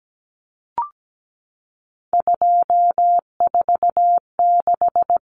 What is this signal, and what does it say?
Morse code, heard as a keyed tone, about 17 wpm, 700 Hz. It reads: 246
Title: Common Cut Numbers - 3-Digit 17wpm